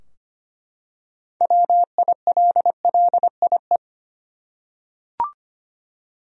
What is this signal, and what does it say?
Morse code, heard as a keyed tone, about 25 wpm, 700 Hz. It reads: WILLIE
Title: USA Common Names - 25wpm